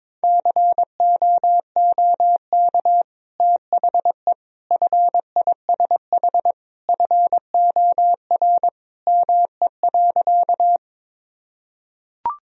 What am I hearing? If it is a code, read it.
Morse code, heard as a keyed tone, about 22 wpm, 700 Hz. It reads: COOK THE FISH FOR ME.
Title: Sentences from Top 1000 Words - Review Part 4 - 22wpm